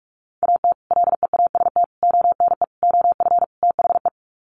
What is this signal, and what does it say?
Morse code, heard as a keyed tone, about 44 wpm, 700 Hz. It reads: AT LEAST ONE OF THE
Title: Sets of 5 Words Encore 45wpm